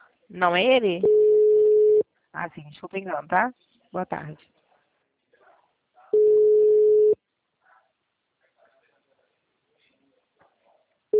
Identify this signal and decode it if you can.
Ringback tone (Europe, 425 Hz), with speech.
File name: out- -2008- - - .20